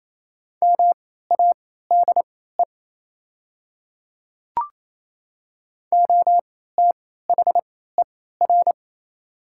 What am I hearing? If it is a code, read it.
Morse code, heard as a keyed tone, about 27 wpm, spaced as for 16 wpm, 700 Hz. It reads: MADE OTHER
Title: 3x Character Spacing - Top 200 Words - 28wpm